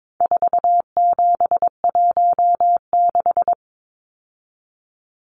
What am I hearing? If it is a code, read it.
Morse code, heard as a keyed tone, about 22 wpm, 700 Hz. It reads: 4716
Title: Numbers - 4-Digit 22wpm